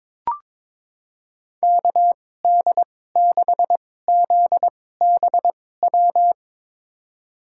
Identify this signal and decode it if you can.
Morse code, heard as a keyed tone, about 22 wpm, spaced as for 16 wpm, 700 Hz. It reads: KD6ZBW
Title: USA Call Signs - 6 Characters - 2x Character Spacing - 22wpm